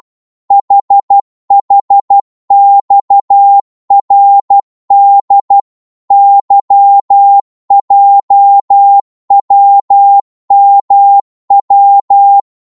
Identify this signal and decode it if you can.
Morse code, heard as a keyed tone, about 12 wpm, 800 Hz. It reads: HHXRD YJWMW
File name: Morse / ltr-12wpm